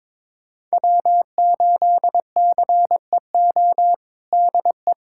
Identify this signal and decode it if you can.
Morse code, heard as a keyed tone, about 22 wpm, 700 Hz. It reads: W8CEO DE
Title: USA Callsign DE Callsign 22wpm